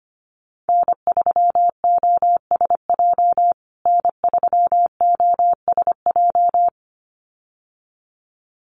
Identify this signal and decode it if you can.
Morse code, heard as a keyed tone, about 25 wpm, 700 Hz. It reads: N3OSJ N3OSJ
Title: USA Call Signs Repeated 2x 25wpm